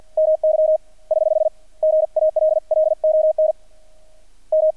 Morse code, heard as a keyed tone, about 50 wpm, 630 Hz. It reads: MY <HH> GALLON M
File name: archived / CRSnet